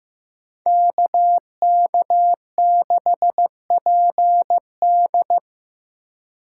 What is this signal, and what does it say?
Morse code, heard as a keyed tone, about 15 wpm, 700 Hz. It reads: KK6PD
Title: USA Call Signs - 4x Character Spacing - 15wpm